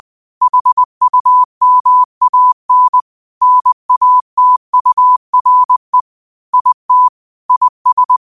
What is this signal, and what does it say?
Morse code, heard as a keyed tone, about 20 wpm, 1000 Hz. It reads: HUMAN NATURE IT IS
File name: qotdfiles / 20wpm